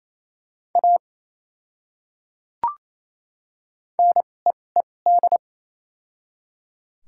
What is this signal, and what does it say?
Morse code, heard as a keyed tone, about 28 wpm, spaced as for 20 wpm, 700 Hz. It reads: A NEED
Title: 2x Character Spacing - Top 500 Words - 28wpm